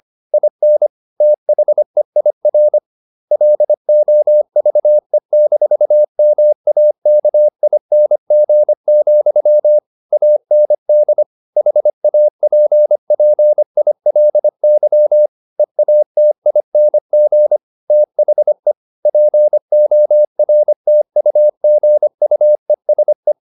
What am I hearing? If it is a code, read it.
Morse code, heard as a keyed tone, about 25 wpm, 600 Hz. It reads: IN THEIR LOVE-MAKING, AND HAPPILY EATING THE PORTUGUESE
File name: the old man and the sea part 1 25wpm